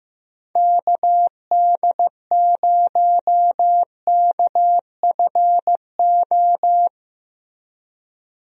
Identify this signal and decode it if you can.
Morse code, heard as a keyed tone, about 15 wpm, 700 Hz. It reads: KD0KFO
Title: USA Call Signs - 6 Characters - Repeated 3x - 15wpm